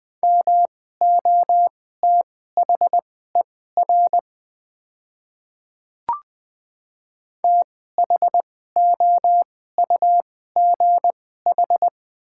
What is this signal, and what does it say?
Morse code, heard as a keyed tone, about 21 wpm, spaced as for 15 wpm, 700 Hz. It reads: MOTHER THOUGH
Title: 2x Character Spacing - Top 500 Words - 20wpm